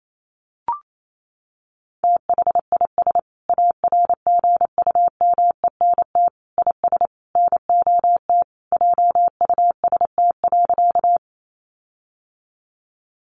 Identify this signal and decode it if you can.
Morse code, heard as a keyed tone, about 28 wpm, 700 Hz. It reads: THIS ARGUMENT IS NOT JUST.
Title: Sentences from Top 1400 Words 28wpm